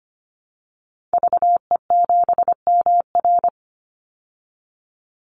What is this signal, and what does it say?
Morse code, heard as a keyed tone, about 25 wpm, 700 Hz. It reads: VE7MR
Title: Call Signs Repeated 3x - 25wpm